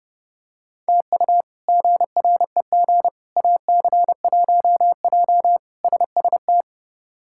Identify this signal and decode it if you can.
Morse code, heard as a keyed tone, about 30 wpm, 700 Hz. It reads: TU GREG AC1J SST
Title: K1USN SST QSO Practice - 30wpm